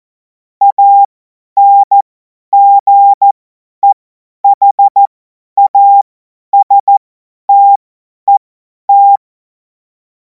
Morse code, heard as a keyed tone, about 14 wpm, spaced as for 10 wpm, 800 Hz. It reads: ANGEHASTET